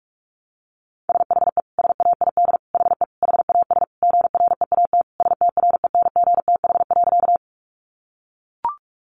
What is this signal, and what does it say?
Morse code, heard as a keyed tone, about 44 wpm, 700 Hz. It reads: SHE SAID HE HAS GREAT STRENGTH.
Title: Sentences from Top 1300 Words 45wpm